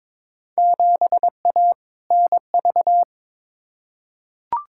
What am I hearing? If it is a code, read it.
Morse code, heard as a keyed tone, about 22 wpm, 700 Hz. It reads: 7A NV
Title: Field Day Exchanges Encore - 22wpm